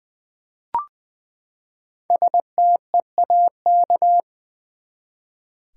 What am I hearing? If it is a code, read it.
Morse code, heard as a keyed tone, about 20 wpm, 700 Hz. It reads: STEAK
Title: Five Letter Words - 20wpm